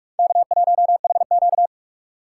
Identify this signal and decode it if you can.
Morse code, heard as a keyed tone, about 58 wpm, spaced as for 51 wpm, 700 Hz. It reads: K1HQ
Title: Speed Racing - USA 1x2 Call Signs 45wpm